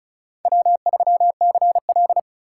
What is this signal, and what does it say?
Morse code, heard as a keyed tone, about 34 wpm, 700 Hz. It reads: W3CL
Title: USA Call Signs - 4 Characters - 3x Character Spacing - 35wpm